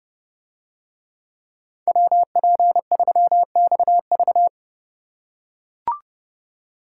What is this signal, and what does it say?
Morse code, heard as a keyed tone, about 29 wpm, 700 Hz. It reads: WP3XV